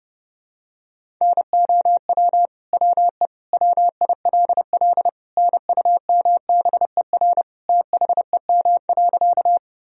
Morse code, heard as a keyed tone, about 29 wpm, 700 Hz. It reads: NOW WE WILL NUMBER THEM.
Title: Sentences from Top 1000 Words - Review Part 1 - 30wpm